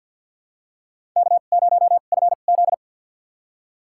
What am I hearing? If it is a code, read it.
Morse code, heard as a keyed tone, about 51 wpm, spaced as for 37 wpm, 700 Hz. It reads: K0FZ
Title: USA Call Signs - 4 Characters - 2x Character Spacing - 50wpm